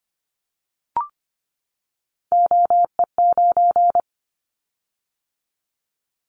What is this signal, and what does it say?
Morse code, heard as a keyed tone, about 25 wpm, 700 Hz. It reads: OE9
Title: Callsign Prefixes for Top 100 DXCC - 25wpm